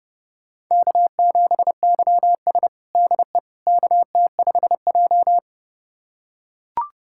Morse code, heard as a keyed tone, about 30 wpm, 700 Hz. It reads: K7YS DE KT5J
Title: USA Callsign DE Callsign - 4 Characters - 30wpm